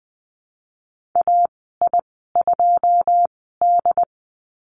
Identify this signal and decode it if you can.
Morse code, heard as a keyed tone, about 20 wpm, spaced as for 15 wpm, 700 Hz. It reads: AI2D
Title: 082 Lesson 14g - Number 2 - Call Signs 20wpm